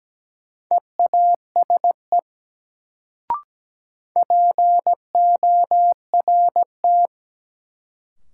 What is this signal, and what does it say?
Morse code, heard as a keyed tone, about 17 wpm, 700 Hz. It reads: EASE PORT
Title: Sentences from Top 1000 Words - Review Part 1 - New Word Drill - 17wpm